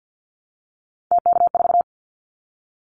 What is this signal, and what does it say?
Morse code, heard as a keyed tone, about 48 wpm, 700 Hz. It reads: TK4
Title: Callsign Prefixes for Top 100 DXCC - 50wpm